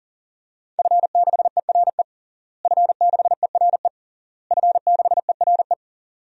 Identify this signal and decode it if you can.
Morse code, heard as a keyed tone, about 40 wpm, 700 Hz. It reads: F6ERE F6ERE F6ERE
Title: DXCC-100 Call Signs - 5 Characters - Repeated 3x - 40wpm